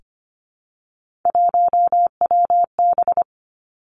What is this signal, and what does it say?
Morse code, heard as a keyed tone, about 25 wpm, 700 Hz. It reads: 1WB
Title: Three Challenging Letter-Number Combinations 25wpm